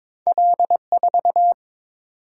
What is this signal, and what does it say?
Morse code, heard as a keyed tone, about 22 wpm, 700 Hz. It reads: L4
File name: practice / 22wpm / Two Letter-Number Combinations - 22wpm